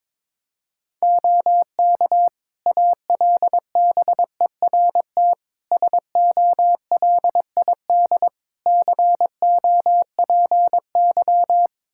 Morse code, heard as a keyed tone, about 21 wpm, 700 Hz. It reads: OK ALBERT SOLID COPY
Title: Practice QSO - Exchange 3 - More Info - 22wpm